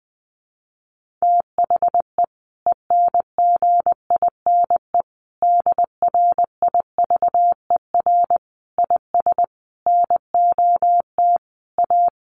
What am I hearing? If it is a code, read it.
Morse code, heard as a keyed tone, about 20 wpm, 700 Hz. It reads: THE ENGINE DRIVER IS NOT A
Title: Sentences from Top 1500 Words 20wpm